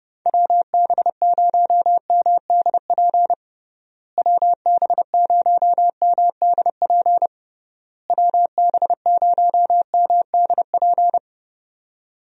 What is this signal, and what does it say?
Morse code, heard as a keyed tone, about 30 wpm, 700 Hz. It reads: WB0MDP WB0MDP WB0MDP
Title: USA Call Signs - 6 Characters - Repeated 3x - 30wpm